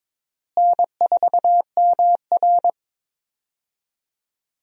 Morse code, heard as a keyed tone, about 22 wpm, 700 Hz. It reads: N4MR
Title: USA Call Signs - 4 Characters - Repeated 3x - 22wpm